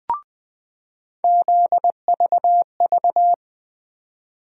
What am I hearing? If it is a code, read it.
Morse code, heard as a keyed tone, about 21 wpm, 700 Hz. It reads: ZVV
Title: USA Call Sign Postfixes - 3 Letters 20wpm